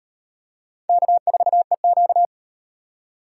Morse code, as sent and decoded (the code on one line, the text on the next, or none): -.- ....- . --.-
K4EQ